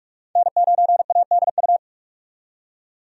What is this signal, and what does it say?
Morse code, heard as a keyed tone, about 43 wpm, 700 Hz. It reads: N9ADU
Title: USA Call Signs - 5 Characters - 45wpm